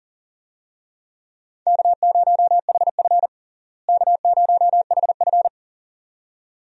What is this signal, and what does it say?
Morse code, heard as a keyed tone, about 37 wpm, 700 Hz. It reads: K0HF K0HF
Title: USA Call Signs - 4 Characters - Repeated 2x - 40wpm